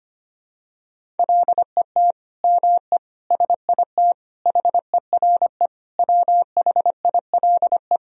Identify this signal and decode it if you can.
Morse code, heard as a keyed tone, about 25 wpm, 700 Hz. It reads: LET ME SIT HERE WHILE
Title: Sentences from Top 1000 Words - Review Part 1 - 25wpm